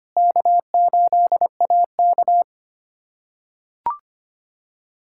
Morse code, heard as a keyed tone, about 25 wpm, 700 Hz. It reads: K8AK
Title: USA Callsign DE Callsign - 4 Characters - 25wpm